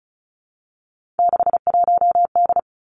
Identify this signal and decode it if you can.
Morse code, heard as a keyed tone, about 36 wpm, 700 Hz. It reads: 61D
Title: Three Challenging Letter-Number Combinations 35wpm